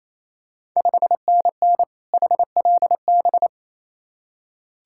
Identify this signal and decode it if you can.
Morse code, heard as a keyed tone, about 28 wpm, 690 Hz. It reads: 5NN HLB
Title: Ontario QSO Party Exchange 28wpm